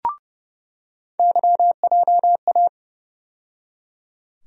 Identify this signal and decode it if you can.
Morse code, heard as a keyed tone, about 32 wpm, 700 Hz. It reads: YJA
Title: USA Call Sign Postfixes - 3 Letters 30wpm